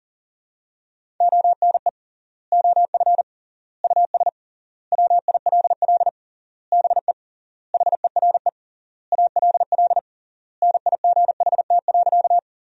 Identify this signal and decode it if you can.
Morse code, heard as a keyed tone, about 37 wpm, 700 Hz. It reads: ONE OF US WILL BE HERE ALL NIGHT.
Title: 3x Word-Spacing - Sentences from Top 300 Words - 40wpm